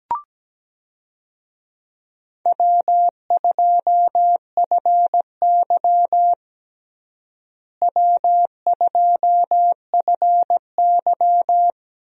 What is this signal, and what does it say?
Morse code, heard as a keyed tone, about 17 wpm, 700 Hz. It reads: W2FY W2FY
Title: USA Call Signs - 4 Characters - Repeated 3x - 17wpm